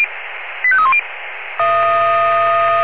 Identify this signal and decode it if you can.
Five-tone selective call (ZVEI), heard as quick runs of five tones, tones of about 70 ms. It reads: 85310
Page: SDR/ZVEI